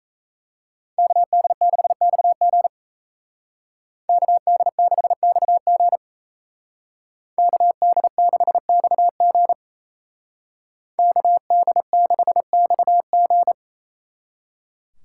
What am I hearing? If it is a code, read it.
Morse code, heard as a keyed tone, about 36 wpm, 700 Hz. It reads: KD6XG KD6XG KD6XG KD6XG